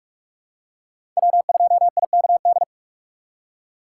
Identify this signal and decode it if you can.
Morse code, heard as a keyed tone, about 43 wpm, 700 Hz. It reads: W2IKD
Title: USA Call Signs - 5 Characters - 4x Character Spacing - 45wpm